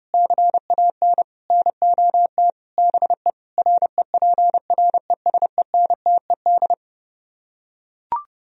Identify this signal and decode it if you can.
Morse code, heard as a keyed tone, about 30 wpm, 700 Hz. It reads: CAN NOT BE REPRESENTED